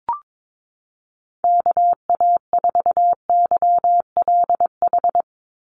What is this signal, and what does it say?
Morse code, heard as a keyed tone, about 22 wpm, 700 Hz. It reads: KA4YLH
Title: USA Call Signs 22wpm